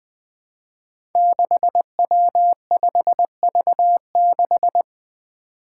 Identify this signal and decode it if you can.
Morse code, heard as a keyed tone, about 20 wpm, 700 Hz. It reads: 6W5V6
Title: Five Challenging Letter-Number Combinations 20wpm